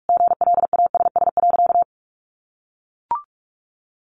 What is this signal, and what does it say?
Morse code, heard as a keyed tone, about 44 wpm, 700 Hz. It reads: GLASS.
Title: Sentences from Top 1400 Words 45wpm